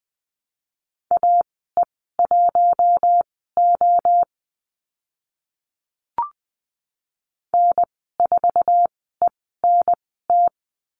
Morse code, heard as a keyed tone, about 20 wpm, spaced as for 14 wpm, 700 Hz. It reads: AE1O N4ENT
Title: 062 Lesson 11g - Letter h - Call Signs 20wpm